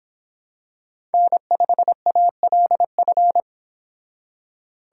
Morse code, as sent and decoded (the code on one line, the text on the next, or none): -. ..... .- .-.. ..-.
N5ALF